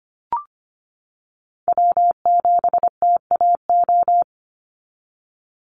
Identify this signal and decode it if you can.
Morse code, heard as a keyed tone, about 25 wpm, 700 Hz. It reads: W7TAO
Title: NA SOTA Chaser Callsigns 25wpm